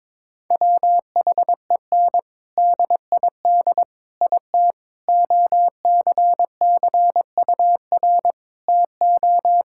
Morse code, heard as a keyed tone, about 22 wpm, 700 Hz. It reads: WHEN DID IT OCCUR TO
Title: Sentences from Top 1000 Words - Review Part 6 - 22wpm